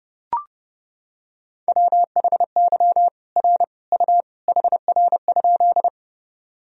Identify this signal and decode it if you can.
Morse code, heard as a keyed tone, about 32 wpm, 700 Hz. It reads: WHY R U HR?
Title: Sentences from Top 200 Words and Abbreviations 30wpm